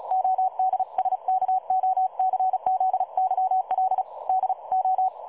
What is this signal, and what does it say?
Morse code, heard as a keyed tone, about 36 wpm, 750 Hz. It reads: ODSKOCZYL DO
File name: dlazaawansowanych / nagraniazpasma / websdr recording 3565.0kHz